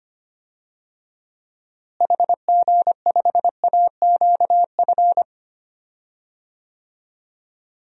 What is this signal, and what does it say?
Morse code, heard as a keyed tone, about 25 wpm, 700 Hz. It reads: HG5AQF
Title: DXCC-100 Call Signs - 6 Characters - Repeated 3x - 25wpm